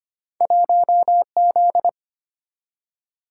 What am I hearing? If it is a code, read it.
Morse code, heard as a keyed tone, about 25 wpm, 700 Hz. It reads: 1Z
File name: Two Letter-Number Combinations - 25wpm